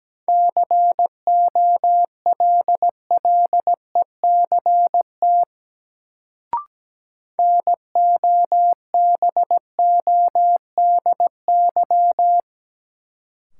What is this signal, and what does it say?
Morse code, heard as a keyed tone, about 17 wpm, 700 Hz. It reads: COLLECT NOBODY
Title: Sentences from Top 1600 Words - New Word Drill - 17wpm